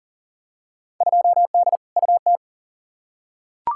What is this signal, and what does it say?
Morse code, heard as a keyed tone, about 40 wpm, 700 Hz. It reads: 2D UT
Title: Field Day Exchanges Encore - 40wpm